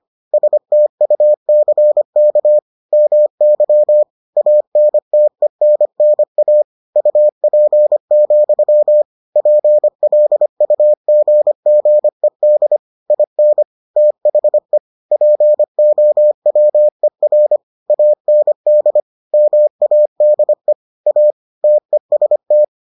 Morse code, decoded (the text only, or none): STUCK MY ANTENNA UP, PLUGGED IN THE POWER AND MADE A TEST